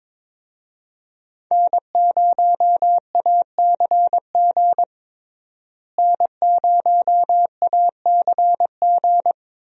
Morse code, heard as a keyed tone, about 22 wpm, 700 Hz. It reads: N0ACG N0ACG
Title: USA Call Signs - 5 Characters - Repeated 3x - 22wpm